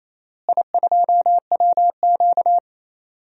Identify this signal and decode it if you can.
Morse code, heard as a keyed tone, about 28 wpm, 700 Hz. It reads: I2WQ
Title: DXCC-100 Call Signs - 4 Characters - Repeated 3x - 28wpm